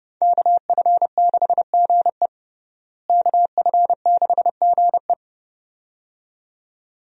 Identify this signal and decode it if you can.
Morse code, heard as a keyed tone, about 29 wpm, 700 Hz. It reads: KF6GE KF6GE
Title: USA Call Signs - 5 Characters - Repeated 2x - 30wpm